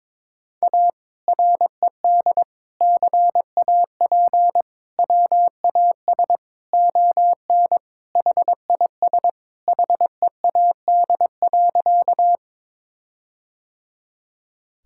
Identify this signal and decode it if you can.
Morse code, heard as a keyed tone, about 22 wpm, 700 Hz. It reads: A RED CAP WAS ON HIS HEAD.
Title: Sentences from Top 1600 Words 22wpm